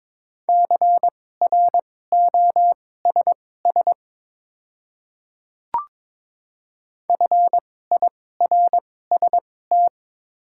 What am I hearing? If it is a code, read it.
Morse code, heard as a keyed tone, about 21 wpm, spaced as for 16 wpm, 700 Hz. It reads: CROSS FIRST
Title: 2x Character Spacing - Top 500 Words - 22wpm